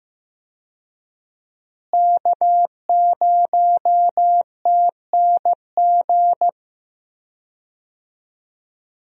Morse code, heard as a keyed tone, about 14 wpm, 700 Hz. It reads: K0TNG
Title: USA Call Signs - 5 Characters - Repeated 2x - 15wpm